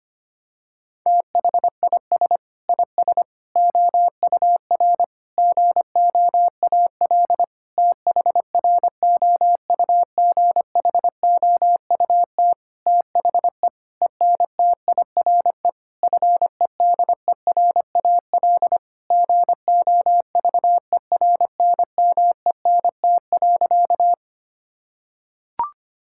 Morse code, heard as a keyed tone, about 25 wpm, 700 Hz. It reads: THIS IS OUR GOAL THROUGHOUT THE ENTIRE FEDERAL GOVERNMENT.
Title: Sentences from Top 1000 Words - Review Part 5 - 25wpm